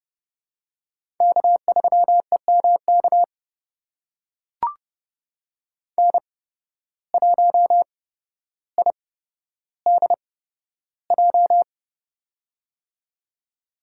Morse code, heard as a keyed tone, about 30 wpm, spaced as for 13 wpm, 700 Hz. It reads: K3EMK N1IDJ
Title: USA Call Signs - 5 Characters - 8x Character Spacing - 30wpm